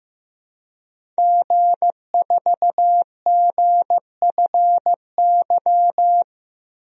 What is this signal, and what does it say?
Morse code, heard as a keyed tone, about 16 wpm, 700 Hz. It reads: G4GFY
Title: DXCC-100 Call Signs - 5 Characters - 15wpm